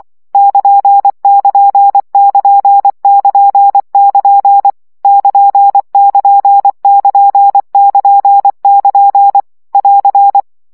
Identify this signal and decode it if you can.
Morse code, heard as a keyed tone, about 24 wpm, 800 Hz. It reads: <KN><KN><KN><KN><KN> <KN><KN><KN><KN><KN> <AR>
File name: CW gyakorlas nyito 20 baud